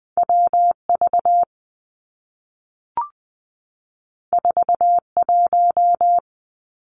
Morse code, heard as a keyed tone, about 20 wpm, 700 Hz. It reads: W V 4 1
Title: Two Challenging Letter-Number Combinations 20wpm